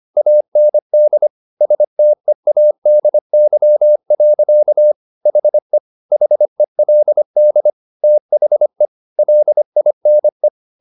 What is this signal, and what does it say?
Morse code, heard as a keyed tone, about 25 wpm, 600 Hz. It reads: AND STEADY. HE HELD THE LINE